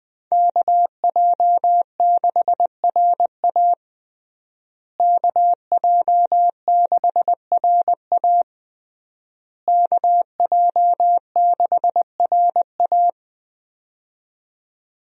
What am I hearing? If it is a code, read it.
Morse code, heard as a keyed tone, about 20 wpm, 700 Hz. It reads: KJ6RA KJ6RA KJ6RA
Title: USA Call Signs - 5 Characters - Repeated 3x - 20wpm